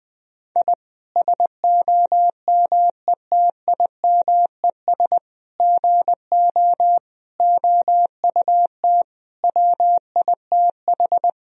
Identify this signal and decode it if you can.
Morse code, heard as a keyed tone, about 20 wpm, 700 Hz. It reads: I SOMETIMES GO OUT WITH
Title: Sentences from Top 1000 Words - Review Part 3 - 20wpm